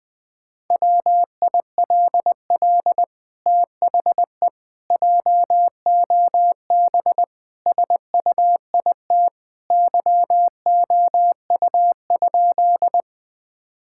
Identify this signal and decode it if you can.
Morse code, heard as a keyed tone, about 20 wpm, 700 Hz. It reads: WILL THE JOB SUIT YOU?
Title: Sentences from Top 1600 Words 20wpm